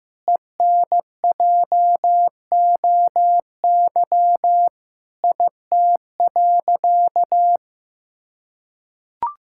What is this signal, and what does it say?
Morse code, heard as a keyed tone, about 16 wpm, 700 Hz. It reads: ENJOY IT.
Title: Sentences from Top 1000 Words - Review Part 5 - 15wpm